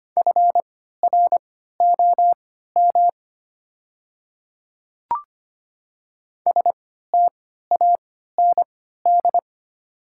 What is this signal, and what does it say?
Morse code, heard as a keyed tone, about 25 wpm, spaced as for 14 wpm, 700 Hz. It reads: FROM STAND
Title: 3x Character Spacing - Top 300 Words - 25wpm